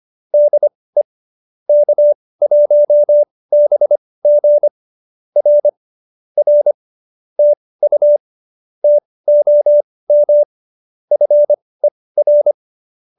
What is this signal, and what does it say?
Morse code, heard as a keyed tone, about 25 wpm, spaced as for 18 wpm, 570 Hz. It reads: DE K1BG R R TU TOM FER
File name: qso201b 18